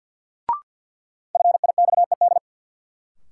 Morse code, heard as a keyed tone, about 47 wpm, 700 Hz. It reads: FIXED
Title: Sentences from Top 1100 Words - New Word Drill - 50wpm